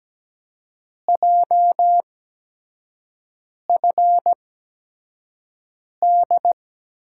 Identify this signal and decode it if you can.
Morse code, heard as a keyed tone, about 17 wpm, spaced as for 5 wpm, 700 Hz. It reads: JFD
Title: USA Call Signs - 8x Character Spacing - 17wpm